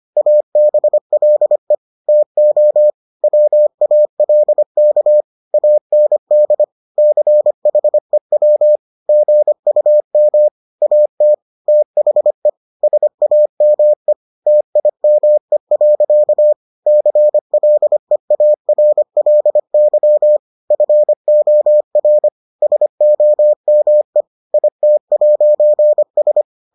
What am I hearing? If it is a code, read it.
Morse code, heard as a keyed tone, about 25 wpm, 600 Hz. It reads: ABLE TO WALK AND CHEW GUM AT THE SAME TIME. CLEARLY FOR SOME IT'S